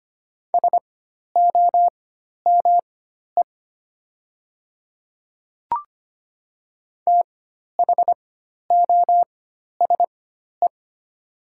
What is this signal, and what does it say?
Morse code, heard as a keyed tone, about 25 wpm, spaced as for 12 wpm, 700 Hz. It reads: SOME THOSE
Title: 4x Character Spacing - Top 300 Words - 25wpm